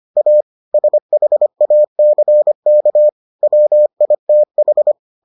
Morse code, heard as a keyed tone, about 25 wpm, 600 Hz. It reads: A SHACK WITH